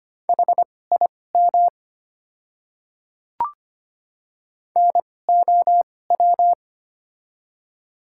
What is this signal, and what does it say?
Morse code, heard as a keyed tone, about 25 wpm, spaced as for 18 wpm, 700 Hz. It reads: HIM NOW
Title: 2x Character Spacing - Top 500 Words - 25wpm